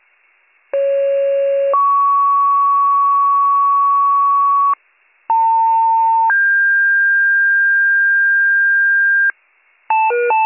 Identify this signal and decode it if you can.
Two-tone page, heard as a one-second tone, then a three-second tone, about 550 then 1120 Hz, about 900 then 1600 Hz, with noise. It is heard from 0.7 to 9.3 s.